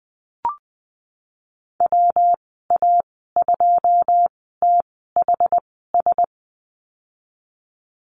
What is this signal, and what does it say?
Morse code, heard as a keyed tone, about 20 wpm, spaced as for 15 wpm, 700 Hz. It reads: WA2THS
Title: 179 Lesson 29g - Slash - Call Signs 20wpm